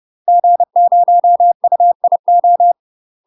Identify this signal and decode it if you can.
Morse code, heard as a keyed tone, about 30 wpm, 700 Hz. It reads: G0UIO
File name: UK MASTER QSOs TRAINING / 30WPM / UK MASTER QSOs 0047